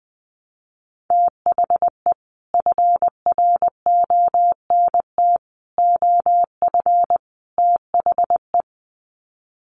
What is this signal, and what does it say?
Morse code, heard as a keyed tone, about 20 wpm, 700 Hz. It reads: THE FRONT OF THE
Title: Sets of 4 Words 20wpm